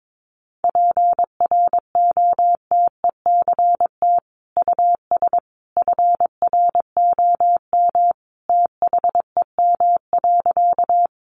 Morse code, heard as a keyed tone, about 22 wpm, 700 Hz. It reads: PROTECT US FROM THEM.
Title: Sentences from Top 900 Words 22wpm